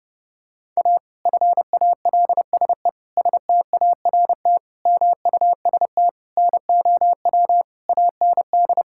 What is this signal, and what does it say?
Morse code, heard as a keyed tone, about 30 wpm, 700 Hz. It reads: A FALSE START MUST NOW AND
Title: Sentences from Top 1600 Words 30wpm